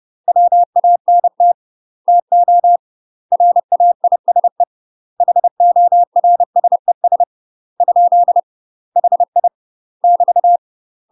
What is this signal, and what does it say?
Morse code, heard as a keyed tone, about 30 wpm, 700 Hz. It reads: WANT TO RAISE HORSES ? HI <BT>
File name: UK MASTER QSOs TRAINING / 30WPM / UK MASTER QSOs 0022